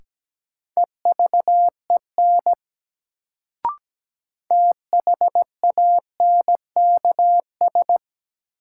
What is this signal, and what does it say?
Morse code, heard as a keyed tone, about 17 wpm, 700 Hz. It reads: EVEN THANKS
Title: Sentences from Top 600 Words - New Word Drill - 17wpm